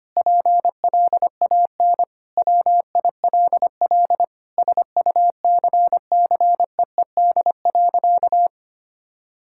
Morse code, heard as a keyed tone, about 25 wpm, 700 Hz. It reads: PLAN WILL SUCCEED.
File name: practice / 25wpm / Sentences with 9-Word Phrases - 25wpm